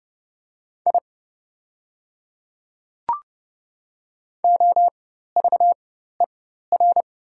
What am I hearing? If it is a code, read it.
Morse code, heard as a keyed tone, about 30 wpm, spaced as for 14 wpm, 690 Hz. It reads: I OVER
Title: 4x Character Spacing - Top 200 Words - 30wpm